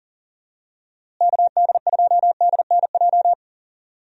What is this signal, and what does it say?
Morse code, heard as a keyed tone, about 40 wpm, 700 Hz. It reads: KD2DNJ
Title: USA Call Signs - 6 Characters - Repeated 2x - 40wpm